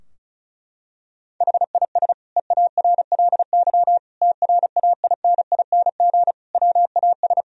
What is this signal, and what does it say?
Morse code, heard as a keyed tone, about 36 wpm, 700 Hz. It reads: HIS EARLY TRAINING WAS
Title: Sentences from Top 1000 Words - Review Part 4 - 35wpm